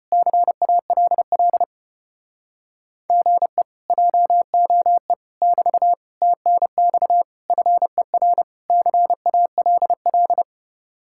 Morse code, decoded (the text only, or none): CALL GE JOE <BT> TNX FER CALL